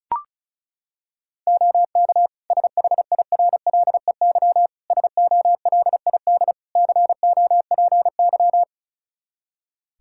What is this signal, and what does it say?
Morse code, heard as a keyed tone, about 36 wpm, 700 Hz. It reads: OK SHIRLEY SOLID COPY
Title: Practice QSO - Exchange 3 - More Info - 35wpm